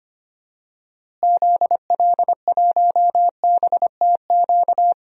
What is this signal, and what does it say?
Morse code, heard as a keyed tone, about 25 wpm, 700 Hz. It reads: ZL1BTQ
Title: DXCC-100 Call Signs - 6 Characters - Repeated 2x - 25wpm